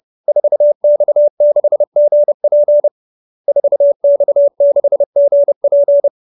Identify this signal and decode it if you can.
Morse code, heard as a keyed tone, about 30 wpm, 570 Hz. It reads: 4X6GP 4X6GP
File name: QSO310 30